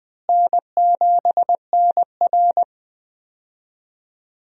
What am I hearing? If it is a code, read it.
Morse code, heard as a keyed tone, about 21 wpm, 700 Hz. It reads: N7NR
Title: USA Call Signs - 4 Characters - Repeated 2x - 20wpm